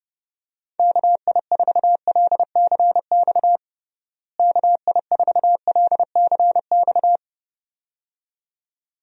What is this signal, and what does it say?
Morse code, heard as a keyed tone, about 30 wpm, 700 Hz. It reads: KI4LCX KI4LCX
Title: USA Call Signs - 6 Characters - Repeated 2x - 30wpm